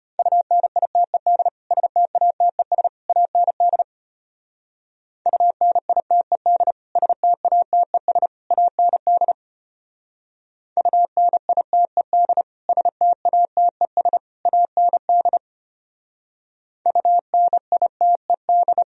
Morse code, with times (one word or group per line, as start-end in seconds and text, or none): 0.2-1.5 UNITED
1.7-2.9 STATES
3.1-3.8 AND
5.3-6.7 UNITED
7.0-8.3 STATES
8.5-9.3 AND
10.8-12.4 UNITED
12.7-14.2 STATES
14.5-15.4 AND
16.9-18.8 UNITED